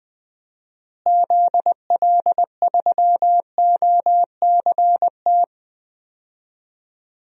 Morse code, heard as a keyed tone, about 20 wpm, 700 Hz. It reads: ZL3OCT
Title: DXCC-100 Call Signs - 6 Characters - Repeated 3x - 20wpm